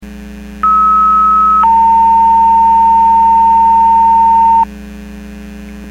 Two-tone page, heard as a one-second tone, then a three-second tone, about 1290 then 910 Hz. It is heard from 0.6 to 4.6 s.
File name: Portage 154.385 Paging